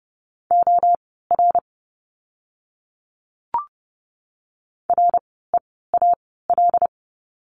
Morse code, heard as a keyed tone, about 30 wpm, spaced as for 17 wpm, 700 Hz. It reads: OR REAL
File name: Lesson 17f - Letter u - Words 30wpm